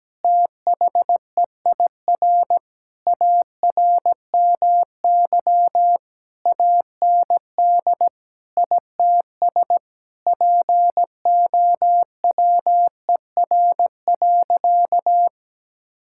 Morse code, heard as a keyed tone, about 17 wpm, 700 Hz. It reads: THEIR ARMY AND ITS POWER.
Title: Sentences from Top 1000 Words - Review Part 5 - 17wpm